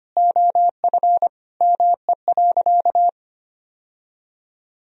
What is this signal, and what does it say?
Morse code, heard as a keyed tone, about 25 wpm, 700 Hz. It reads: OF ME.
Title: Sentences from Top 1000 Words - Review Part 4 - 25wpm